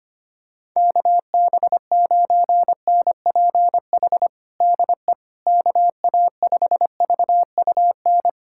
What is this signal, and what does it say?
Morse code, heard as a keyed tone, about 25 wpm, 700 Hz. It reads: KB9NPH DE KA5VUN
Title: USA Callsign DE Callsign 25wpm